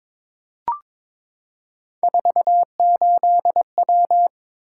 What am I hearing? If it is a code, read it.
Morse code, heard as a keyed tone, about 22 wpm, 700 Hz. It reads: 48W
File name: practice / 22wpm / Three Letter-Number Combinations - 22wpm